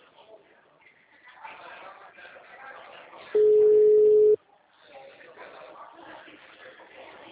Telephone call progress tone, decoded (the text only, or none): ringback tone